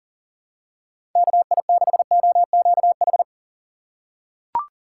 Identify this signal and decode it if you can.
Morse code, heard as a keyed tone, about 37 wpm, 700 Hz. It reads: KI6OQH
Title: USA Call Signs - 2x Character Spacing - 40wpm